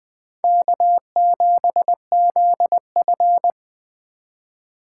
Morse code, heard as a keyed tone, about 20 wpm, 700 Hz. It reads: K7ZF
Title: Speed Racing - USA 1x2 Call Signs 17wpm